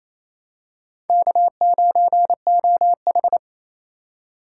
Morse code, heard as a keyed tone, about 28 wpm, 700 Hz. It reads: K9OH
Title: USA Call Signs - 4 Characters - 3x Character Spacing - 28wpm